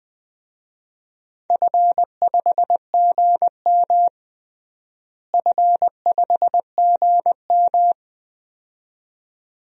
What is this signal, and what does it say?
Morse code, heard as a keyed tone, about 20 wpm, 700 Hz. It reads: F5GM F5GM
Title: DXCC-100 Call Signs - 4 Characters - Repeated 2x - 20wpm